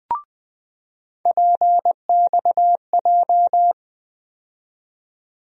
Morse code, heard as a keyed tone, about 20 wpm, 700 Hz. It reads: PXJ
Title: USA Call Sign Postfixes - 3 Letters 20wpm